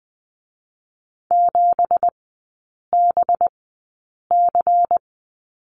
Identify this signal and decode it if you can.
Morse code, heard as a keyed tone, about 20 wpm, spaced as for 8 wpm, 700 Hz. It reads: 7BC
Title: 184 Lesson 30e - Letter b - Three Characters 20wpm